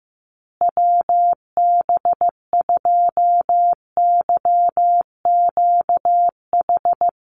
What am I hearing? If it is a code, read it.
Morse code, heard as a keyed tone, about 15 wpm, 700 Hz. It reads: WB2YQH
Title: Call Signs Repeated 2x - 15wpm